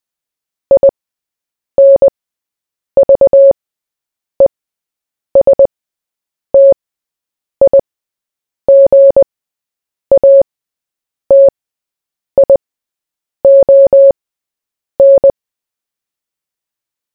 Morse code, heard as a keyed tone, about 20 wpm, spaced as for 8 wpm, 550 Hz. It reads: INVESTIGATION
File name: Short-Story-4-8wpm